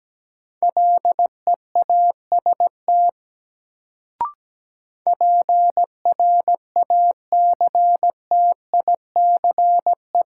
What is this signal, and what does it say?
Morse code, heard as a keyed tone, about 17 wpm, 700 Hz. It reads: LEAST PRACTICE
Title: Sentences from Top 700 Words - New Word Drill - 17wpm